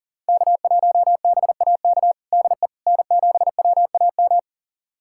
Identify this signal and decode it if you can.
Morse code, heard as a keyed tone, about 40 wpm, 700 Hz. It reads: K1BAK DE N7WAM
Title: USA Callsign DE Callsign - 5 Characters - 40wpm